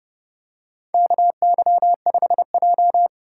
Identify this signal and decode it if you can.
Morse code, heard as a keyed tone, about 30 wpm, 700 Hz. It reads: KY5J
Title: USA Call Signs - 4 Characters - 3x Character Spacing - 30wpm